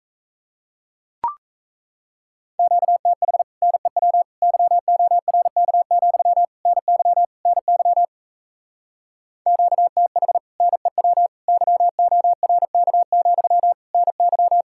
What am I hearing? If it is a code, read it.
Morse code, heard as a keyed tone, about 38 wpm, 700 Hz. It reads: QTH NEW YORK, NY NY QTH NEW YORK, NY